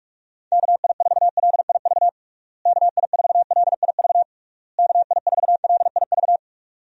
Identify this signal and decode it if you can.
Morse code, heard as a keyed tone, about 41 wpm, 700 Hz. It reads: KI4LIV KI4LIV KI4LIV
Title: USA Call Signs - 6 Characters - Repeated 3x - 45wpm